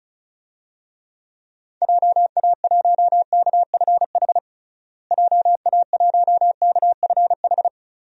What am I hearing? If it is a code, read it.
Morse code, heard as a keyed tone, about 36 wpm, 700 Hz. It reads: JA1KFH JA1KFH
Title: DXCC-100 Call Signs - 6 Characters - Repeated 3x - 35wpm